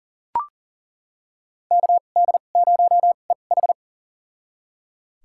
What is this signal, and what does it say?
Morse code, heard as a keyed tone, about 40 wpm, spaced as for 29 wpm, 700 Hz. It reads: KD0EH
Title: USA Call Signs - 5 Characters - 2x Character Spacing - 40wpm